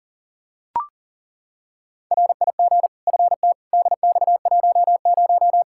Morse code, heard as a keyed tone, about 40 wpm, 700 Hz. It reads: RIG FT DX10
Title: Practice QSO - Exchange 2 - Info - 40wpm